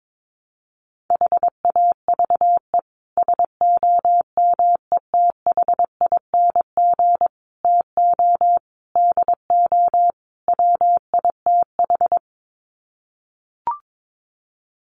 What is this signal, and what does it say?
Morse code, heard as a keyed tone, about 22 wpm, 700 Hz. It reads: HAVE SOMETHING TO DO WITH
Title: Sets of 5 Words Encore 22wpm